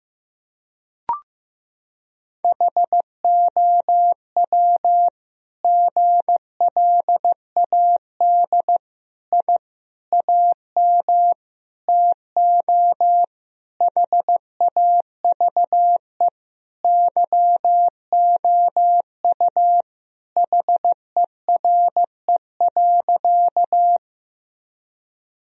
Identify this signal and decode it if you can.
Morse code, heard as a keyed tone, about 15 wpm, 700 Hz. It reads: HOW GLAD I AM TO HAVE YOU HERE.
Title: Sentences from Top 1000 Words - Review Part 2 - 15wpm